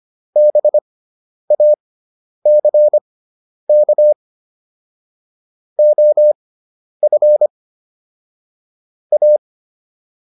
Morse code, heard as a keyed tone, about 25 wpm, spaced as for 10 wpm, 600 Hz. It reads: BACK OF A